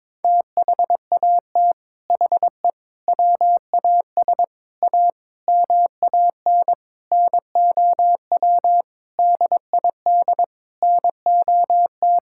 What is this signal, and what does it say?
Morse code, heard as a keyed tone, about 22 wpm, 700 Hz. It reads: THAT HE WAS A MAN NOW DID NOT
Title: Sentences from Top 1000 Words - Review Part 6 - 22wpm